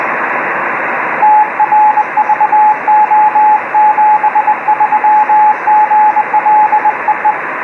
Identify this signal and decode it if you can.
Morse code, heard as a keyed tone, about 21 wpm, 850 Hz, with radio noise. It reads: TRVO 73 GL EE